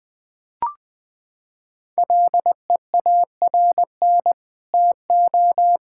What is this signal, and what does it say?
Morse code, heard as a keyed tone, about 21 wpm, 700 Hz. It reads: LEARN TO
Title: Sentences with 9-Word Phrases - 20wpm